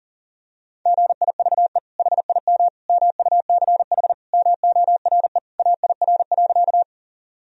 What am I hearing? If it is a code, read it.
Morse code, heard as a keyed tone, about 37 wpm, 700 Hz. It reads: GIVE HIM MUCH MORE AIR.
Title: Sentences from Top 1000 Words - Review Part 1 - 40wpm